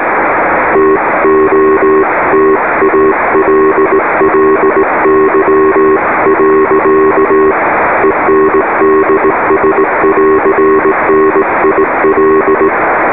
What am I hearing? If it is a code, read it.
Morse code, heard as a keyed tone, about 18 wpm, 390 Hz, with radio noise. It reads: TOTALLY. ENDS <AR> NIL